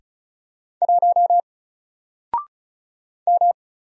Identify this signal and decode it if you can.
Morse code, heard as a keyed tone, about 33 wpm, 700 Hz. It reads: 1 M
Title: Single Letter-Number Alt 35wpm